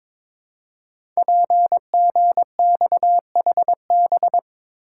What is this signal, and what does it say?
Morse code, heard as a keyed tone, about 22 wpm, 700 Hz. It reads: PGXHB
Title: Code Groups - 5 Letters - 22wpm